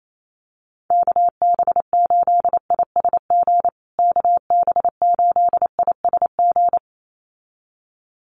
Morse code, heard as a keyed tone, about 28 wpm, 700 Hz. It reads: KB8ISG KB8ISG
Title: USA Call Signs Repeated 2x 28wpm